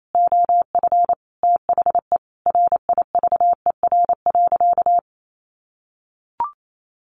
Morse code, heard as a keyed tone, about 28 wpm, 700 Hz. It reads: OF THE RIVER.
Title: Sentences from Top 1300 Words 28wpm